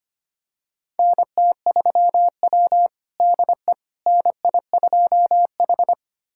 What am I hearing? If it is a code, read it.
Morse code, heard as a keyed tone, about 25 wpm, 700 Hz. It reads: NT3W DE NI2H
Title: USA Callsign DE Callsign - 4 Characters - 25wpm